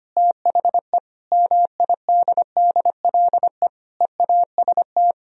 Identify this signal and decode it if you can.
Morse code, heard as a keyed tone, about 33 wpm, 700 Hz. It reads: THE MIDDLE EAST